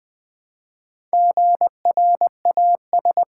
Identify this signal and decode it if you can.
Morse code, heard as a keyed tone, about 20 wpm, 700 Hz. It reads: GRAS
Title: Build A Word - 1 Letter At A Time - 5-Letter Words - 20wpm